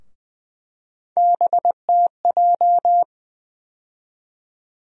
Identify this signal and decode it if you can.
Morse code, heard as a keyed tone, about 20 wpm, 700 Hz. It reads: BTJ
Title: USA Call Sign Postfixes - 3 Letters 20wpm